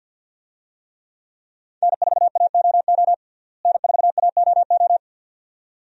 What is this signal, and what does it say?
Morse code, heard as a keyed tone, about 51 wpm, 700 Hz. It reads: N4AOO N4AOO
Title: USA Call Signs - 5 Characters - Repeated 2x - 50wpm